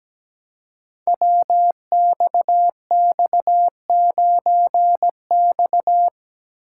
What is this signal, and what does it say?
Morse code, heard as a keyed tone, about 17 wpm, 700 Hz. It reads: WXX9X
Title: Code Groups - 5 Letter:Numbers - 17wpm